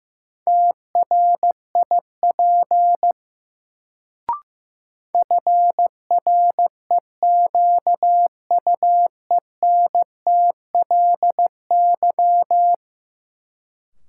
Morse code, heard as a keyed tone, about 15 wpm, 700 Hz. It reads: TRIP FREQUENTLY
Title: Sentences from Top 1300 Words - New Word Drill Alt - 15wpm